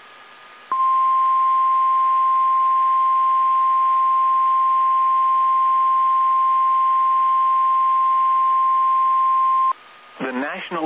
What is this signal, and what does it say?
Emergency Alert System (SAME) tones (the 1050 Hz attention tone), with speech. It is heard from 0.7 to 9.7 s.